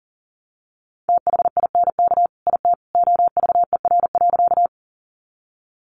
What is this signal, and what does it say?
Morse code, heard as a keyed tone, about 40 wpm, 700 Hz. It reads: THINK IT OVER.
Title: Sentences from Top 200 Words 40wpm